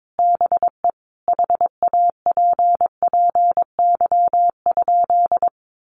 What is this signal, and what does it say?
Morse code, heard as a keyed tone, about 22 wpm, 700 Hz. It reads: BE HAPPY?
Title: Sentences from Top 1100 Words 22wpm